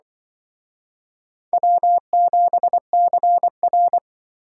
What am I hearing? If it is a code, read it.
Morse code, heard as a keyed tone, about 24 wpm, 700 Hz. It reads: W7CR
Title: Speed Racing - USA 1x2 Call Signs 20wpm